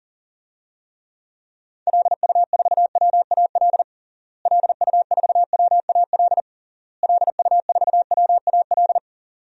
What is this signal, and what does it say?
Morse code, heard as a keyed tone, about 40 wpm, 700 Hz. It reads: LU4WAL LU4WAL LU4WAL
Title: DXCC-100 Call Signs - 6 Characters - Repeated 3x - 40wpm